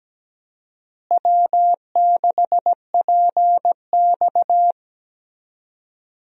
Morse code, heard as a keyed tone, about 17 wpm, 700 Hz. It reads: W6PX
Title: USA Call Signs - 4 Characters - Repeated 3x - 17wpm